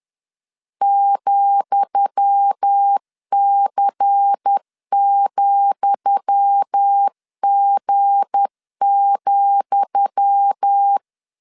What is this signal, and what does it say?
Morse code, heard as a keyed tone, about 10 wpm, 800 Hz. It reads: ,C,G,
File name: LEZIONI / 10 SIGNS / 10D COMA 10wpm 3013